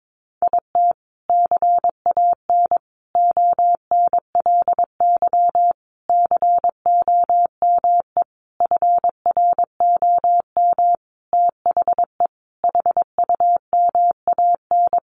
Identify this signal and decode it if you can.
Morse code, heard as a keyed tone, about 22 wpm, 700 Hz. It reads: IT CAN ONLY COME FROM THE HUMAN
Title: Sentences from Top 1400 Words 22wpm